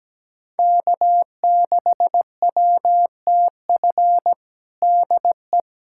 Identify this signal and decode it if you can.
Morse code, heard as a keyed tone, about 17 wpm, 700 Hz. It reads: K6WTF DE
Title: USA Callsign DE Callsign - 5 Characters - 17wpm